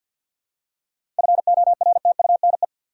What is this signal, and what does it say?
Morse code, heard as a keyed tone, about 46 wpm, 700 Hz. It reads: FORTUNE
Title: Seven Letter Words - 50wpm